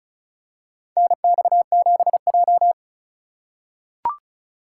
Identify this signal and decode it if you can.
Morse code, heard as a keyed tone, about 36 wpm, 700 Hz. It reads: NX7J
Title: USA Call Signs - 4 Characters - 2x Character Spacing - 35wpm